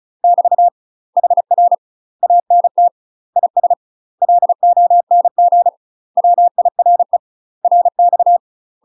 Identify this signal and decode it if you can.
Morse code, heard as a keyed tone, about 34 wpm, 700 Hz. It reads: <BT> HR ANT IS LONG WIRE RX
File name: UK MASTER QSOs TRAINING / 35WPM / UK MASTER QSOs 0008